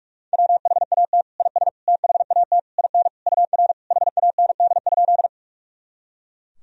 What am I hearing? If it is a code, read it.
Morse code, heard as a keyed tone, about 46 wpm, 700 Hz. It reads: WHAT IS THAT IN UR HAND?
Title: Sentences from Top 200 Words and Abbreviations 45wpm